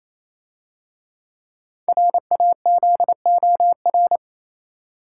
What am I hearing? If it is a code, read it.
Morse code, heard as a keyed tone, about 28 wpm, 700 Hz. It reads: RAZOR
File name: Easy Words - Speed-Racing with Hints - 22wpm